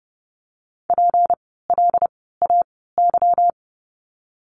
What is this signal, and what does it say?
Morse code, heard as a keyed tone, about 30 wpm, spaced as for 17 wpm, 700 Hz. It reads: PLAY
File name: course / 30:10wpm / 237 Lesson 38f - Letter q - Words 30wpm